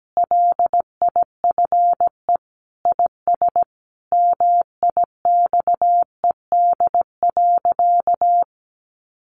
Morse code, heard as a keyed tone, about 17 wpm, 700 Hz. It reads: LIFE IS MIXED.
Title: Sentences from Top 1500 Words 17wpm